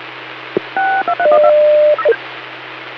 DTMF, heard as telephone touch tones, tones of about 80 ms. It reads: 623